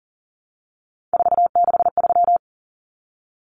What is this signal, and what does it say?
Morse code, heard as a keyed tone, about 40 wpm, 700 Hz. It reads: 463
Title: Common Cut Numbers - 3-Digit 40wpm